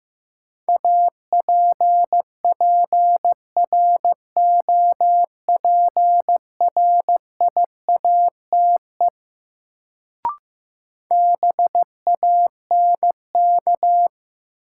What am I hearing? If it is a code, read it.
Morse code, heard as a keyed tone, about 15 wpm, 700 Hz. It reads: APPROPRIATE BANK
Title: Sentences from Top 900 Words - New Word Drill - 15wpm